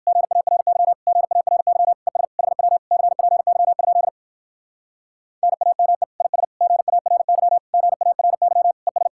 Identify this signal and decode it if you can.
Morse code, heard as a keyed tone, about 59 wpm, 700 Hz. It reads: GARY GARY ES HW CPY? NAME IS GARY GARY ES